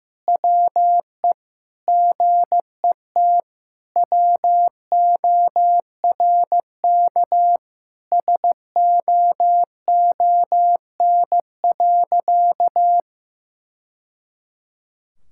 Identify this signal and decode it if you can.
Morse code, heard as a keyed tone, about 15 wpm, 700 Hz. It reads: WE GET WORK SOON.
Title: Sentences from Top 1000 Words - Review Part 6 - 15wpm